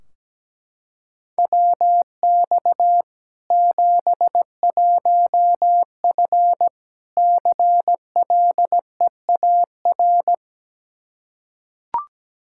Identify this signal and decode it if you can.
Morse code, heard as a keyed tone, about 17 wpm, 700 Hz. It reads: WX 71F CLEAR
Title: K1USN SST QSO Practice 17wpm